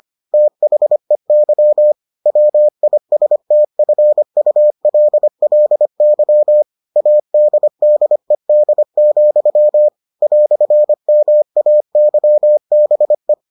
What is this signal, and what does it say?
Morse code, heard as a keyed tone, about 25 wpm, 600 Hz. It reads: THEY WISTFULLY ADDED, "MAYBE